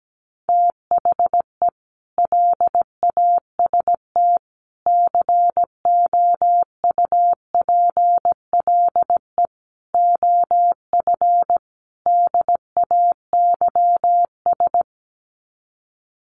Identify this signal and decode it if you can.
Morse code, heard as a keyed tone, about 17 wpm, 700 Hz. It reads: THE LAST COUPLE OF DAYS
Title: Sets of 5 Words Encore 17wpm